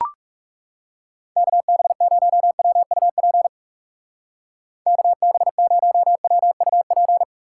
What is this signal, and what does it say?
Morse code, heard as a keyed tone, about 35 wpm, 700 Hz. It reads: KB0WUP KB0WUP
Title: Speed Racing - USA 2x3 Call Signs 30wpm